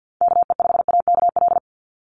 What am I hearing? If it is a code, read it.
Morse code, heard as a keyed tone, about 50 wpm, 700 Hz. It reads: KE5AKL
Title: NA SOTA Activator Callsigns 50wpm